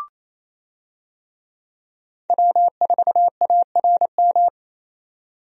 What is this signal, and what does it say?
Morse code, heard as a keyed tone, about 28 wpm, 700 Hz. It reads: W4ARM
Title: USA Call Signs - 5 Characters - 28wpm